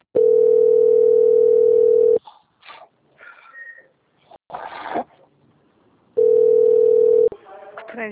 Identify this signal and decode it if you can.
Ringback tone (North America), 440 + 480 Hz, with speech.
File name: out- - - - - .32861